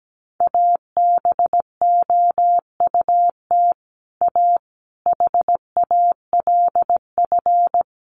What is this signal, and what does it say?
Morse code, heard as a keyed tone, about 17 wpm, 700 Hz. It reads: ABOUT A HALF
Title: Sentences from Top 1400 Words 17wpm